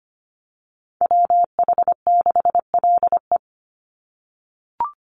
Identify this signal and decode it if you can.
Morse code, heard as a keyed tone, about 25 wpm, 700 Hz. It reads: WH6LE
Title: NA SOTA Chaser Callsigns 25wpm